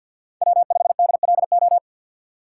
Morse code, heard as a keyed tone, about 47 wpm, 700 Hz. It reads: W5DLO
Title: USA Call Signs - 6x Character Spacing - 50wpm